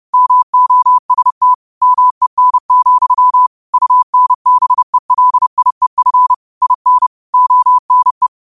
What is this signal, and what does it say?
Morse code, heard as a keyed tone, about 30 wpm, 1000 Hz. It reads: MOST MEN, UNBELIEF IN ONE
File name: qotdfiles / 30wpm